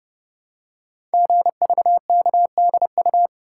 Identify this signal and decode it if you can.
Morse code, heard as a keyed tone, about 30 wpm, 700 Hz. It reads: GVKDU
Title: Code Groups - 5 Letter:Numbers - 30wpm